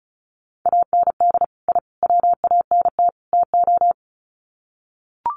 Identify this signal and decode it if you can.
Morse code, heard as a keyed tone, about 34 wpm, 700 Hz. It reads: AND I WANT TO
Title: Sets of 4 Words 35wpm